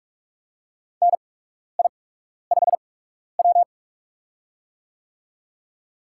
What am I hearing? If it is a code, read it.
Morse code, heard as a keyed tone, about 46 wpm, spaced as for 12 wpm, 700 Hz. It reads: NI5W
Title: USA Call Signs - 4 Characters - 8x Character Spacing - 45wpm